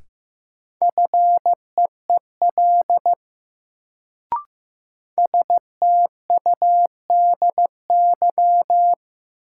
Morse code, heard as a keyed tone, about 15 wpm, 700 Hz. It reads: FEEL STUDY
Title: Sets of 5 Words - New Word Drill - 15wpm